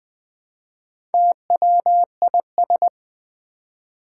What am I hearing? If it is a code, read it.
Morse code, heard as a keyed tone, about 20 wpm, 700 Hz. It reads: TWIS
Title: Build A Word - 1 Letter At A Time - 5-Letter Words - 20wpm